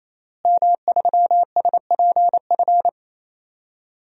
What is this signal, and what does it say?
Morse code, heard as a keyed tone, about 27 wpm, 700 Hz. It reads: M3SPF
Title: DXCC-100 Call Signs - 5 Characters - Repeated 3x - 28wpm